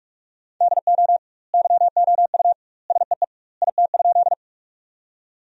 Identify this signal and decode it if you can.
Morse code, heard as a keyed tone, about 42 wpm, 700 Hz. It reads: DO YOU SEE IT?
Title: 2x Word-Spacing - Sentences from Top 100 Words - 45wpm